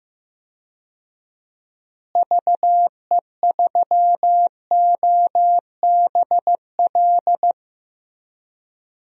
Morse code, heard as a keyed tone, about 15 wpm, 700 Hz. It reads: VE3OBL
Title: DXCC-100 Call Signs - 6 Characters - Repeated 2x - 15wpm